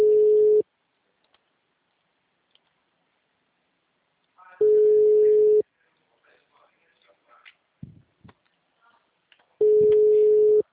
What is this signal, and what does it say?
Ringback tone (Europe, 425 Hz), with speech.